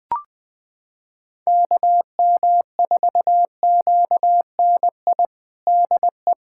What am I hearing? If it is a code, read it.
Morse code, heard as a keyed tone, about 20 wpm, 700 Hz. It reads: KM4QNI DE
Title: USA Callsign DE Callsign - 6 Characters - 20wpm